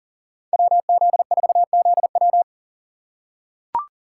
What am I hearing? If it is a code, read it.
Morse code, heard as a keyed tone, about 40 wpm, 700 Hz. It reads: WZ4ZW
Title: USA Call Signs - 6x Character Spacing - 40wpm